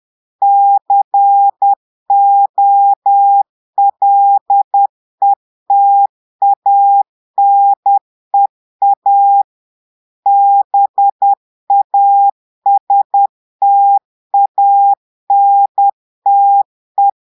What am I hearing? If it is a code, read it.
Morse code, heard as a keyed tone, about 10 wpm, 800 Hz. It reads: COLETANEA BASTANTE
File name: Portugais - texte en clair 1 - vitesse - 10 mots minute